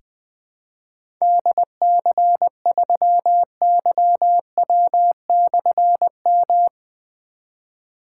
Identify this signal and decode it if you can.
Morse code, heard as a keyed tone, about 20 wpm, 700 Hz. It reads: DC3YW/M
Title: Difficult Call Signs 20wpm